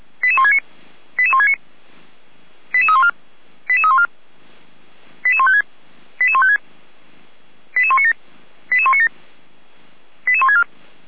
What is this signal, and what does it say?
Five-tone selective call (ZVEI), heard as quick runs of five tones, tones of about 70 ms. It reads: 88169 88169 88314 88314 88156 88156 88197 88197 88164